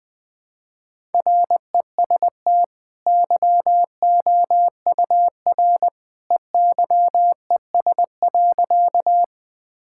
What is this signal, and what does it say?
Morse code, heard as a keyed tone, about 20 wpm, 700 Hz. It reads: REST YOUR EYES.
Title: Easy Sentences with Hints - 20wpm